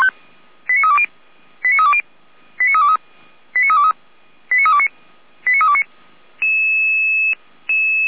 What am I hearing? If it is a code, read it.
Five-tone selective call (ZVEI), heard as quick runs of five tones, tones of about 70 ms. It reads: 89320 89320 89323 89323 89329 89329